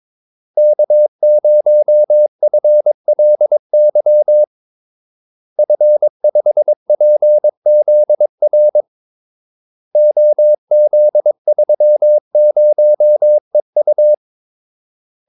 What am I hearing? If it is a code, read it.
Morse code, heard as a keyed tone, about 22 wpm, 600 Hz. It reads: K0FLY F5PZR OZ30EU
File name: Callsigns-22wpm